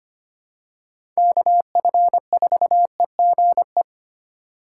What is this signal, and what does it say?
Morse code, heard as a keyed tone, about 25 wpm, 700 Hz. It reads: KF4EGE
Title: USA Call Signs - 6 Characters - 3x Character Spacing - 25wpm